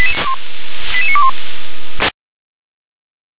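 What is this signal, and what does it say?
Five-tone selective call (ZVEI), heard as quick runs of five tones, tones of about 70 ms. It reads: 88921
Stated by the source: ZVEI five-tone selective call, received by SDR.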